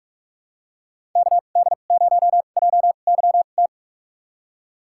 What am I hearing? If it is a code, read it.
Morse code, heard as a keyed tone, about 44 wpm, spaced as for 32 wpm, 700 Hz. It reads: KD0JYT
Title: USA Call Signs - 6 Characters - 2x Character Spacing - 45wpm